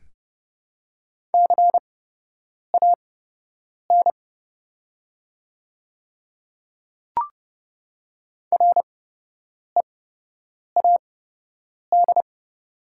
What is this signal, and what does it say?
Morse code, heard as a keyed tone, about 30 wpm, spaced as for 8 wpm, 700 Hz. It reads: CAN READ
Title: 8x Character Spacing - Top 200 Words - 30wpm